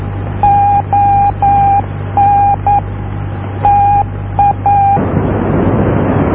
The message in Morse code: ON TA